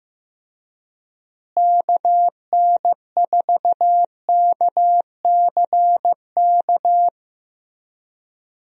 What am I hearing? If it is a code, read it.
Morse code, heard as a keyed tone, about 15 wpm, 700 Hz. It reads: KN4KCK
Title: USA Call Signs - 6 Characters - Repeated 2x - 15wpm